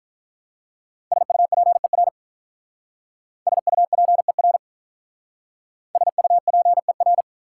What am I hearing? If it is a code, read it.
Morse code, heard as a keyed tone, about 46 wpm, 700 Hz. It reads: SUPER SUPER SUPER